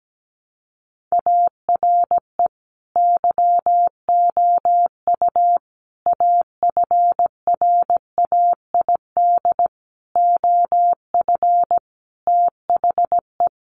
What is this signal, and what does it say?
Morse code, heard as a keyed tone, about 17 wpm, 700 Hz. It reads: ARE YOU AFRAID OF THE
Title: Sentences from Top 1500 Words 17wpm